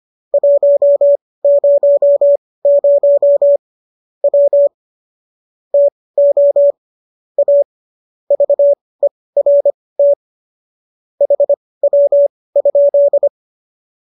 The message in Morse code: 100 W TO A VERT HW?

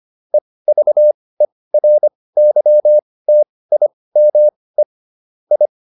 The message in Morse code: EVERYTIME I

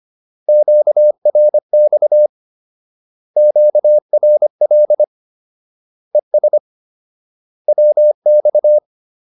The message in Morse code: QRX QRL ES WX

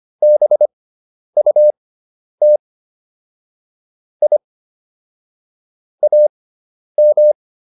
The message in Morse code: BUT I AM

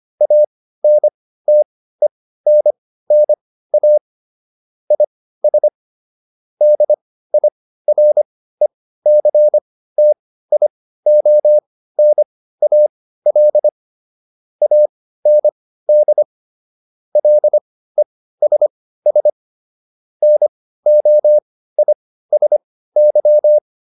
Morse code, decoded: ANTENNA IS DIRECTIONAL AND LESS NOISY

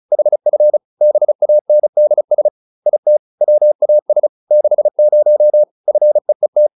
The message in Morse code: HF BANDS IT WAS 60 FEET

